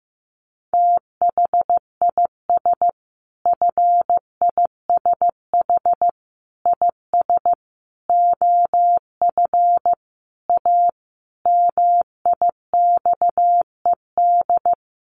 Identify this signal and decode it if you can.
Morse code, heard as a keyed tone, about 15 wpm, 700 Hz. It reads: THIS FISH IS OF A MIXED
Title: Sentences from Top 1500 Words 15wpm